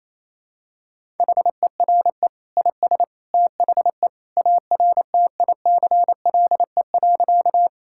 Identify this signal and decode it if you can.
Morse code, heard as a keyed tone, about 27 wpm, 700 Hz. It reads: HERE IS THE ARTICLE.
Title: Sentences from Top 1000 Words - Review Part 5 - 28wpm